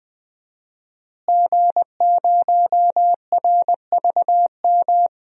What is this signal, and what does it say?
Morse code, heard as a keyed tone, about 20 wpm, 700 Hz. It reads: G0RVM
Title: DXCC-100 Call Signs - 5 Characters - Repeated 3x - 20wpm